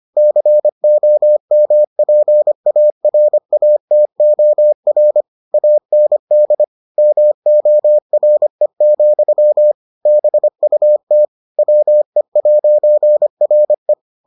Morse code, heard as a keyed tone, about 25 wpm, 600 Hz. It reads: COMPARATOR AND MORE, BUT WE'RE